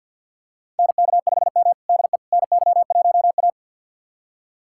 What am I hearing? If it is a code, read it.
Morse code, heard as a keyed tone, about 46 wpm, 700 Hz. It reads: NK5M DE NY1A
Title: USA Callsign DE Callsign - 4 Characters - 50wpm